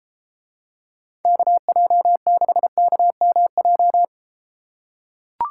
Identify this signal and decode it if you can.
Morse code, heard as a keyed tone, about 32 wpm, 700 Hz. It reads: KJ6KMJ